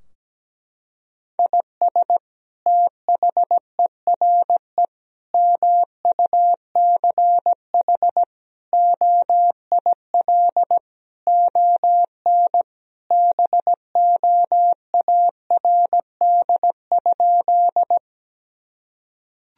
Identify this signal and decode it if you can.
Morse code, heard as a keyed tone, about 17 wpm, 700 Hz. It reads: IS THERE MUCH OIL ON BOARD?
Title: Sentences from Top 1000 Words - Review Part 4 - 17wpm